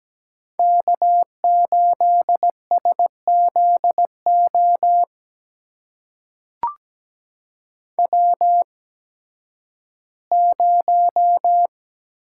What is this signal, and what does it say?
Morse code, heard as a keyed tone, about 17 wpm, spaced as for 11 wpm, 700 Hz. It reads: K8SZO W0
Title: USA Call Signs - 5 Characters - 8x Character Spacing - 17wpm